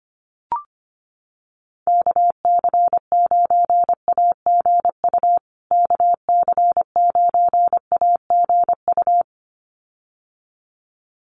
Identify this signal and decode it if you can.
Morse code, heard as a keyed tone, about 25 wpm, 700 Hz. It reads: KC9AGU KC9AGU
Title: USA Call Signs Repeated 2x 25wpm